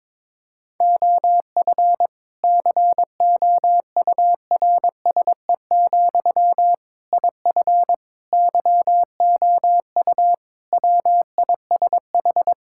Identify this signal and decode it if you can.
Morse code, heard as a keyed tone, about 22 wpm, 700 Hz. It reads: OF COURSE, IF YOU WISH
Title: Sentences from Top 1600 Words 22wpm